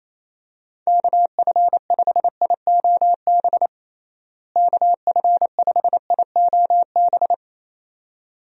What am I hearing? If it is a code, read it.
Morse code, heard as a keyed tone, about 28 wpm, 700 Hz. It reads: KF5IOB KF5IOB
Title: USA Call Signs - 6 Characters - Repeated 2x - 28wpm